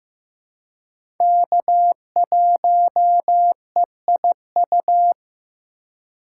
Morse code, heard as a keyed tone, about 15 wpm, 700 Hz. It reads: K1EIU
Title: USA Call Signs - 5 Characters - 3x Character Spacing - 15wpm